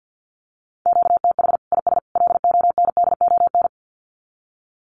Morse code, heard as a keyed tone, about 50 wpm, 700 Hz. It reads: QTH IS LONDON
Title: Common QSO Phrases 50wpm